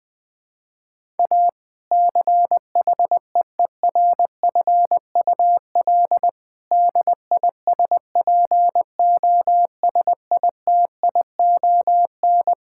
Morse code, heard as a keyed tone, about 20 wpm, 700 Hz. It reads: A CHEERFUL DISPOSITION
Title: Difficult Sentences with Hints - 20wpm